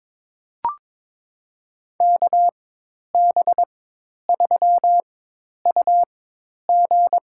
Morse code, heard as a keyed tone, about 22 wpm, spaced as for 10 wpm, 700 Hz. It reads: KB3UG
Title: USA Call Signs - 4x Character Spacing - 22wpm